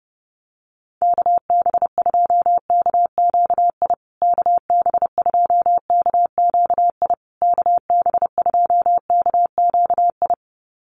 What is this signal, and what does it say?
Morse code, heard as a keyed tone, about 30 wpm, 700 Hz. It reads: KB2KQI KB2KQI KB2KQI
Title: USA Call Signs Repeated 3x 30wpm